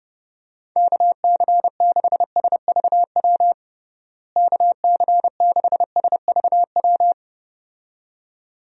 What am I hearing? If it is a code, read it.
Morse code, heard as a keyed tone, about 32 wpm, 700 Hz. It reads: KC6SVW KC6SVW
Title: USA Call Signs - 6 Characters - Repeated 2x - 30wpm